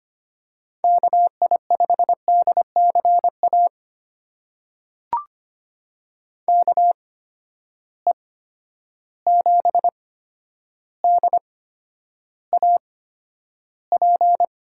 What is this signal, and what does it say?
Morse code, heard as a keyed tone, about 25 wpm, spaced as for 11 wpm, 700 Hz. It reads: KI5DCA KE7DAP
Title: USA Call Signs - 6 Characters - 8x Character Spacing - 25wpm